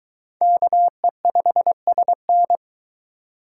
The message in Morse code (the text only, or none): KE5SN